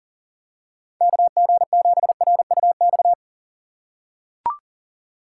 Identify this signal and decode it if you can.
Morse code, heard as a keyed tone, about 39 wpm, 700 Hz. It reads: KG7RUX
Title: USA Call Signs - 6 Characters - 4x Character Spacing - 40wpm